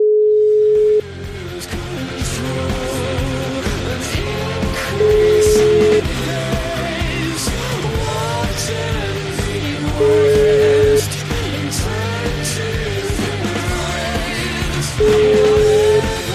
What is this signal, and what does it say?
Ringback tone (Europe, 425 Hz), with noise.